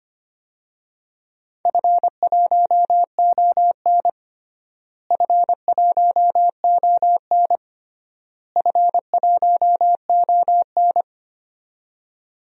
Morse code, heard as a keyed tone, about 25 wpm, 700 Hz. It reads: F1ON F1ON F1ON
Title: DXCC-100 Call Signs - 4 Characters - Repeated 3x - 25wpm